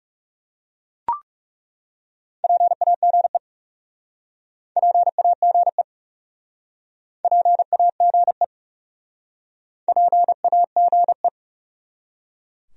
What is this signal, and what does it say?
Morse code, heard as a keyed tone, about 38 wpm, 700 Hz. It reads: PAGE PAGE PAGE PAGE